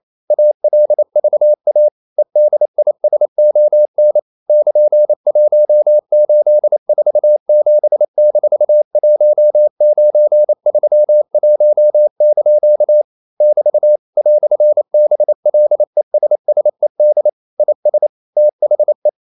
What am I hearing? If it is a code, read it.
Morse code, heard as a keyed tone, about 28 wpm, 600 Hz. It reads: ALVA EDISON <KN>1847-1931) <BT> "BLESSED IS THE
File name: CW / Practice / EnglishQuotations0228